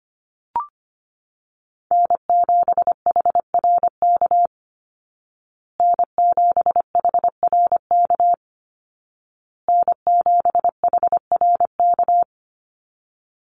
Morse code, heard as a keyed tone, about 25 wpm, 700 Hz. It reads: N7HRK N7HRK N7HRK
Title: Call Signs Repeated 2x - 25wpm